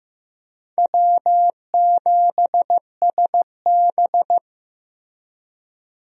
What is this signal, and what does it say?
Morse code, heard as a keyed tone, about 15 wpm, 700 Hz. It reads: W7SB
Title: USA Call Signs - 4 Characters - Repeated 2x - 15wpm